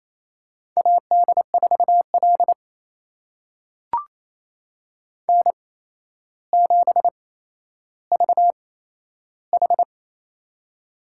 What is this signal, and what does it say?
Morse code, heard as a keyed tone, about 27 wpm, spaced as for 12 wpm, 700 Hz. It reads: AD4L N7VH
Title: USA Call Signs - 4 Characters - 8x Character Spacing - 28wpm